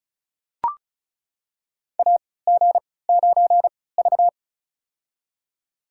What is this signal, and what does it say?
Morse code, heard as a keyed tone, about 36 wpm, spaced as for 20 wpm, 700 Hz. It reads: AG9V
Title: USA Call Signs - 4 Characters - 3x Character Spacing - 35wpm